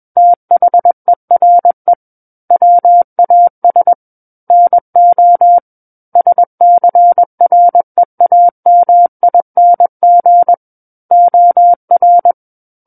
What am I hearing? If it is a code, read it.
Morse code, heard as a keyed tone, about 21 wpm, 700 Hz. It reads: THERE WAS NO SCREAMING OR
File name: Book One / 12-Chapter 12 - 21 WPM